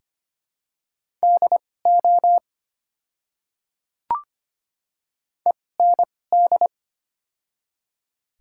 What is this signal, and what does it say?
Morse code, heard as a keyed tone, about 25 wpm, spaced as for 18 wpm, 700 Hz. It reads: DO END
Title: 2x Character Spacing - Top 200 Words - 25wpm